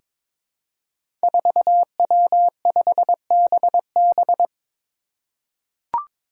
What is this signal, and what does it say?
Morse code, heard as a keyed tone, about 22 wpm, 700 Hz. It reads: 4W5BB
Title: Five Challenging Letter-Number Combinations 22wpm